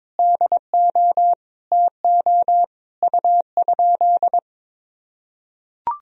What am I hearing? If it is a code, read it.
Morse code, heard as a keyed tone, about 22 wpm, 700 Hz. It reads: DO TO U?
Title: Sentences from Top 100 Words and Abbreviations 22wpm